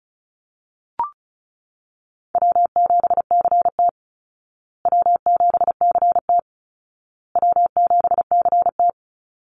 Morse code, heard as a keyed tone, about 34 wpm, 700 Hz. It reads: W7CT W7CT W7CT
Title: Easy Call Signs Repeated 3x - 35wpm